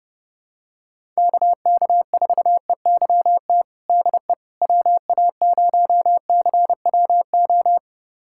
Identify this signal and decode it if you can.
Morse code, heard as a keyed tone, about 30 wpm, 700 Hz. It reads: KK4EYT DE WA0CWO
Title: USA Callsign DE Callsign - 6 Characters - 30wpm